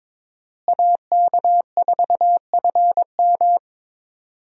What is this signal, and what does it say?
Morse code, heard as a keyed tone, about 22 wpm, 700 Hz. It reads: AK4FM
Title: USA Call Signs - 5 Characters - Repeated 2x - 22wpm